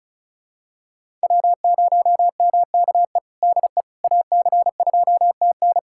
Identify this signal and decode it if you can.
Morse code, heard as a keyed tone, about 34 wpm, 700 Hz. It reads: W0MKE DE AC2TN
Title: USA Callsign DE Callsign - 5 Characters - 35wpm